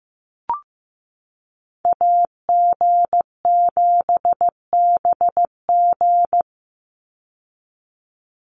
Morse code, heard as a keyed tone, about 15 wpm, 700 Hz. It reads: AG7BG
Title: USA Call Signs 15wpm